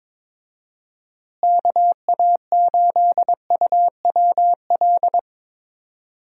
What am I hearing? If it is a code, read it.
Morse code, heard as a keyed tone, about 22 wpm, 700 Hz. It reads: KA8UWL
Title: USA Call Signs - 6 Characters - Repeated 3x - 22wpm